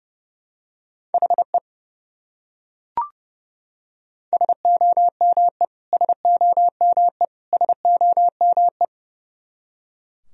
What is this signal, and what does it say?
Morse code, heard as a keyed tone, about 30 wpm, 700 Hz. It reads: HE SOME SOME SOME
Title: Top 100 Words Repeated 30wpm